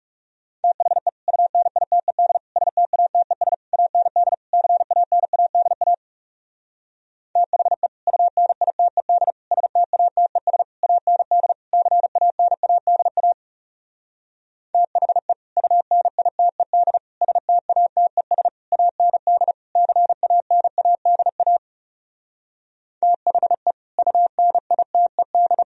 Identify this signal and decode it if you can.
Morse code, heard as a keyed tone, about 40 wpm, 700 Hz. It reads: THE UNITED STATES AND CANADA THE UNITED STATES AND CANADA THE UNITED STATES AND CANADA THE UNITED